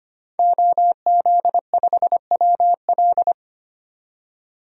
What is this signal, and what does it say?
Morse code, heard as a keyed tone, about 25 wpm, 700 Hz. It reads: OZ5WL
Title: DXCC-100 Call Signs - 5 Characters - 25wpm